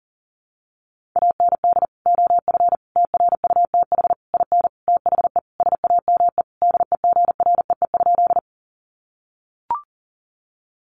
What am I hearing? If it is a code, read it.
Morse code, heard as a keyed tone, about 40 wpm, 700 Hz. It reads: AND OF TRUTH IN THE SAME DEGREE?
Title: Sentences from Top 1500 Words 40wpm